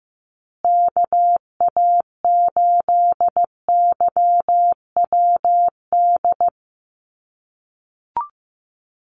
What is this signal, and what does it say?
Morse code, heard as a keyed tone, about 15 wpm, 700 Hz. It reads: KA8YWD
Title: USA Call Signs Repeated 2x 15wpm